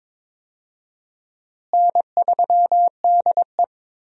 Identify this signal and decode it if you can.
Morse code, heard as a keyed tone, about 22 wpm, 700 Hz. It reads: N3DE
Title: USA Call Signs - 4 Characters - Repeated 3x - 22wpm